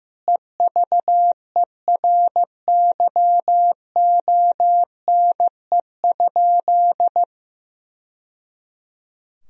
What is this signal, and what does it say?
Morse code, heard as a keyed tone, about 15 wpm, 700 Hz. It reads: EVERYONE?
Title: Sentences from Top 1000 Words - Review Part 4 - 15wpm